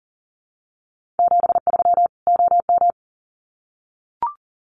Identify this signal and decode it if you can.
Morse code, heard as a keyed tone, about 40 wpm, 700 Hz. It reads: 73 OM
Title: Common QSO Phrases 40wpm